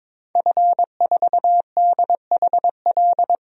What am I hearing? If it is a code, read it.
Morse code, heard as a keyed tone, about 22 wpm, 700 Hz. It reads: F4DHL
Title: DXCC-100 Call Signs - 5 Characters - 22wpm